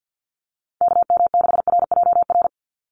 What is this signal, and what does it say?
Morse code, heard as a keyed tone, about 50 wpm, 700 Hz. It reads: KM6RJR
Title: USA Call Signs 50wpm